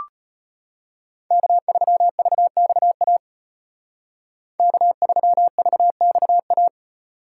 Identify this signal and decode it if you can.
Morse code, heard as a keyed tone, about 30 wpm, 700 Hz. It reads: K3VXA K3VXA
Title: Speed Racing - USA 1x3 Call Signs 25wpm